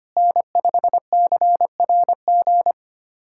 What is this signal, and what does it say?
Morse code, heard as a keyed tone, about 25 wpm, 700 Hz. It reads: N5CRG
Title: USA Callsign DE Callsign - 5 Characters - 25wpm